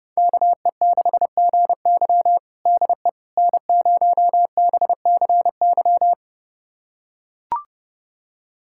Morse code, heard as a keyed tone, about 30 wpm, 700 Hz. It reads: KE6GY DE N0BCY
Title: USA Callsign DE Callsign 30wpm